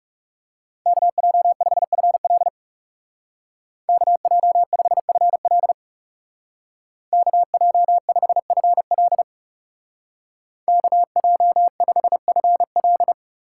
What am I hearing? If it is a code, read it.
Morse code, heard as a keyed tone, about 34 wpm, 700 Hz. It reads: KJ5FL KJ5FL KJ5FL KJ5FL
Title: Speed Racing - USA 2x2 Call Signs 30wpm